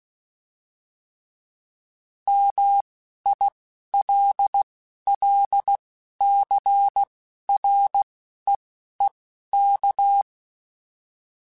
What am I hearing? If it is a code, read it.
Morse code, heard as a keyed tone, about 16 wpm, spaced as for 11 wpm, 800 Hz. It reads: MILLCREEK